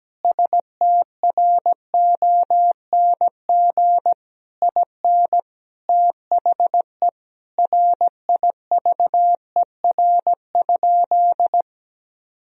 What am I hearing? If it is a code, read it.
Morse code, heard as a keyed tone, about 17 wpm, 700 Hz. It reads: STRONG IN THE RIVER?
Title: Sentences from Top 1000 Words - Review Part 4 - 17wpm